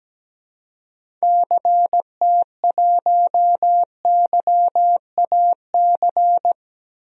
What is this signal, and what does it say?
Morse code, heard as a keyed tone, about 17 wpm, 700 Hz. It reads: CT1YAC
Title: DXCC-100 Call Signs - 6 Characters - Repeated 2x - 17wpm